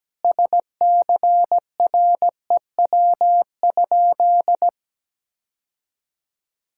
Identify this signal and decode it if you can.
Morse code, heard as a keyed tone, about 17 wpm, 700 Hz. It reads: SCREW?
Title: Sentences from Top 1600 Words 17wpm